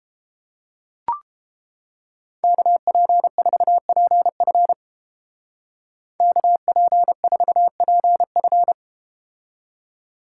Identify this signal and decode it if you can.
Morse code, heard as a keyed tone, about 30 wpm, 700 Hz. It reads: KP4PF KP4PF